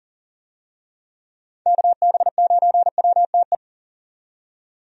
Morse code, heard as a keyed tone, about 40 wpm, 700 Hz. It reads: KB9WTE
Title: USA Call Signs - 6 Characters - 40wpm